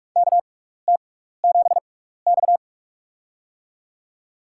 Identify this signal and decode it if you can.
Morse code, heard as a keyed tone, about 46 wpm, spaced as for 16 wpm, 700 Hz. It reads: KT7X
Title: USA Call Signs - 4 Characters - 6x Character Spacing - 45wpm